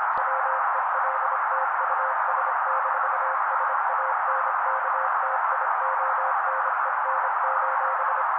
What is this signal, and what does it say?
Morse code, heard as a keyed tone, about 26 wpm, 530 Hz. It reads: WELT US-SANKTIONEN:
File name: daily / 25wpm noise